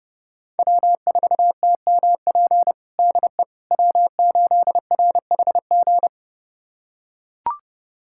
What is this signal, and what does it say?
Morse code, heard as a keyed tone, about 30 wpm, 700 Hz. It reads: W4TMP DE W8RHG
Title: USA Callsign DE Callsign - 5 Characters - 30wpm